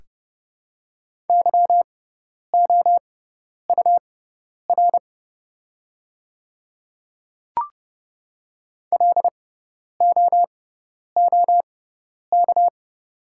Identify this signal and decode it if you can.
Morse code, heard as a keyed tone, about 30 wpm, spaced as for 10 wpm, 690 Hz. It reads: YOUR LOOK
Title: 6x Character Spacing - Top 200 Words - 30wpm